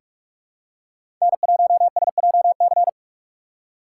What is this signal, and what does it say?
Morse code, heard as a keyed tone, about 46 wpm, 700 Hz. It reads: N1SJC
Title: USA Call Signs - 5 Characters - 4x Character Spacing - 45wpm